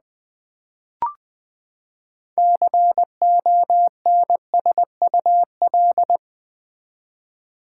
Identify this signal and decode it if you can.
Morse code, heard as a keyed tone, about 20 wpm, 700 Hz. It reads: CONSUL
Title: Build A Word - 6 Letters At A Time - 12-Letter Words - 20wpm